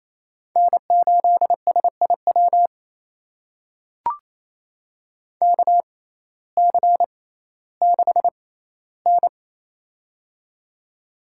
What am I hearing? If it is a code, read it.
Morse code, heard as a keyed tone, about 28 wpm, spaced as for 15 wpm, 700 Hz. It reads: N8SIW KC6N
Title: USA Call Signs - 6x Character Spacing - 28wpm